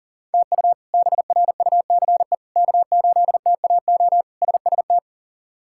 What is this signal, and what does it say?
Morse code, heard as a keyed tone, about 40 wpm, 700 Hz. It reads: TU BRUCE K8TAO SST
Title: K1USN SST QSO Practice - 40wpm